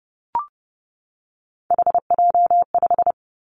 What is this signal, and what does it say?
Morse code, heard as a keyed tone, about 30 wpm, 700 Hz. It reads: HJ5
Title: Callsign Prefixes for Top 100 DXCC - 30wpm